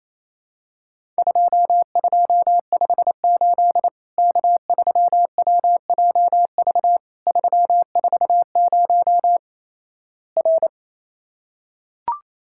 Morse code, heard as a keyed tone, about 28 wpm, 700 Hz. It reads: 2258 K3WJV 340 R
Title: Worked All Europe DX Contest - QTC Exchange - 28wpm